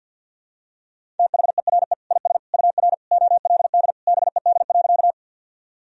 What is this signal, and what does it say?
Morse code, heard as a keyed tone, about 46 wpm, 700 Hz. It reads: THERE IS UR OLD BED.
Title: Sentences from Top 500 Words and Abbreviations 50wpm